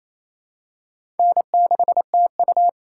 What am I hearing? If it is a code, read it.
Morse code, heard as a keyed tone, about 28 wpm, 700 Hz. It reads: N6TU
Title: USA Call Signs - 4 Characters - 2x Character Spacing - 28wpm